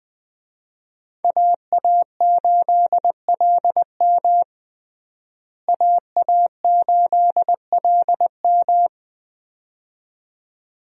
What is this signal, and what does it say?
Morse code, heard as a keyed tone, about 20 wpm, 700 Hz. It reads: AA8LM AA8LM
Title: USA Call Signs - 5 Characters - Repeated 2x - 20wpm